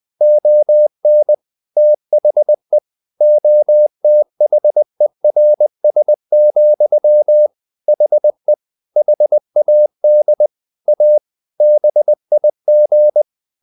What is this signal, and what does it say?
Morse code, heard as a keyed tone, about 20 wpm, 600 Hz. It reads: ON THE OTHERS, HE HAD A BIG
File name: the old man and the sea / the old man and the sea part 1 20wpm